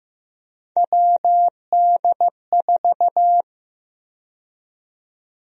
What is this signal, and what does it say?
Morse code, heard as a keyed tone, about 15 wpm, 700 Hz. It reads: WD4
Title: USA Call Sign Prefixes - 15wpm